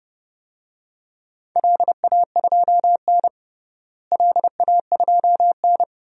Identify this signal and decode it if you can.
Morse code, heard as a keyed tone, about 30 wpm, 700 Hz. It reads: LA2N LA2N
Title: DXCC-100 Call Signs - 4 Characters - Repeated 2x - 30wpm